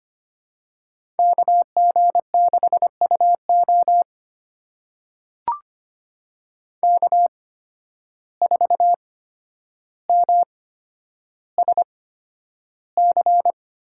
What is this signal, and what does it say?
Morse code, heard as a keyed tone, about 25 wpm, spaced as for 11 wpm, 700 Hz. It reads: KG6UO K4MSC
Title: USA Call Signs - 5 Characters - 8x Character Spacing - 25wpm